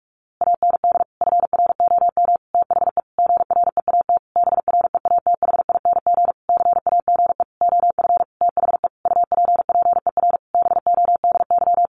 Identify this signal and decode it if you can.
Morse code, heard as a keyed tone, about 44 wpm, 700 Hz. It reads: AND FROM THE GREAT BREATHING CAGE OF THE UPPER BODY
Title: A Sound Of Thunder 45wpm